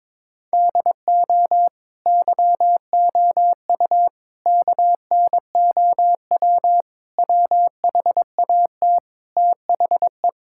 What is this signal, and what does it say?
Morse code, heard as a keyed tone, about 21 wpm, 700 Hz. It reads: DO YOU KNOW WHAT THE
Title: Sentences from Top 1600 Words 22wpm